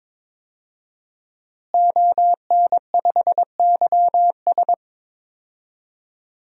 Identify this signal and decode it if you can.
Morse code, heard as a keyed tone, about 22 wpm, 700 Hz. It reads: ON5YS
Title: DXCC-100 Call Signs - 5 Characters - Repeated 3x - 22wpm